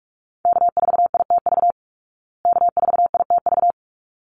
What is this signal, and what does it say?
Morse code, heard as a keyed tone, about 44 wpm, 700 Hz. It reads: K4ITV K4ITV
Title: Call Signs Repeated 3x - 45wpm